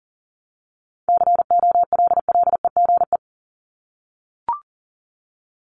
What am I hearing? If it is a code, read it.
Morse code, heard as a keyed tone, about 40 wpm, 700 Hz. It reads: COLLEGE
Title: Top 1001-1250 Words 40wpm